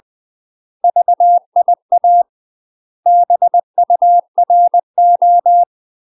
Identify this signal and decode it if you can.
Morse code, heard as a keyed tone, about 20 wpm, 700 Hz. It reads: VIA BURO
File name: UK MASTER QSOs TRAINING / 20WPM / UK MASTER QSOs 0011